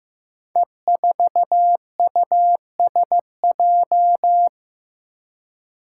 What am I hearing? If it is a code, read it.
Morse code, heard as a keyed tone, about 15 wpm, 700 Hz. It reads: E4USJ
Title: Five Challenging Letter-Number Combinations 15wpm